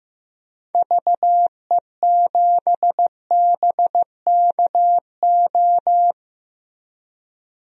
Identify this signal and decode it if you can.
Morse code, heard as a keyed tone, about 15 wpm, 700 Hz. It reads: VE7BKO
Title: DXCC-100 Call Signs - 6 Characters - Repeated 3x - 15wpm